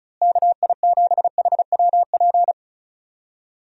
Morse code, heard as a keyed tone, about 34 wpm, 700 Hz. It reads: KI7HWP
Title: USA Call Signs - 6 Characters - Repeated 3x - 35wpm